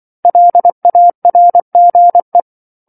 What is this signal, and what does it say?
Morse code, heard as a keyed tone, about 24 wpm, 700 Hz. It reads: LARGE
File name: 07-Chapter-07 24 WPM